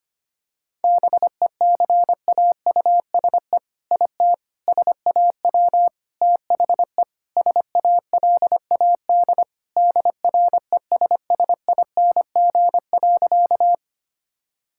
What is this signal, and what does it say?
Morse code, heard as a keyed tone, about 25 wpm, 700 Hz. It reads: BECAUSE IT SAW THE SALAD DRESSING.
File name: practice / 25wpm / All-Age Appropriate Jokes - 25wpm